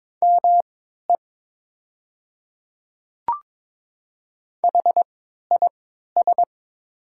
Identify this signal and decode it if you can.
Morse code, heard as a keyed tone, about 22 wpm, spaced as for 13 wpm, 690 Hz. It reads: ME HIS
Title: 3x Character Spacing - Top 300 Words - 22wpm